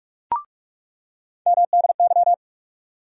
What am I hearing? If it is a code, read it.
Morse code, heard as a keyed tone, about 42 wpm, 700 Hz. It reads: MDY
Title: USA Call Sign Postfixes - 3 Letters 45wpm